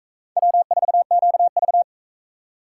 Morse code, heard as a keyed tone, about 37 wpm, 700 Hz. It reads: W4QV